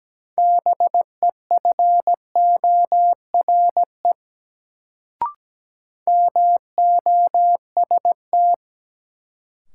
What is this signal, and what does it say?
Morse code, heard as a keyed tone, about 17 wpm, 700 Hz. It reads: BEFORE MOST
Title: Sentences from Top 200 Words - New Word Drill - 17wpm